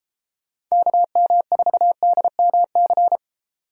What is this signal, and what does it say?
Morse code, heard as a keyed tone, about 33 wpm, 700 Hz. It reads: KM4DMC
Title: Speed Racing - USA 2x3 Call Signs 28wpm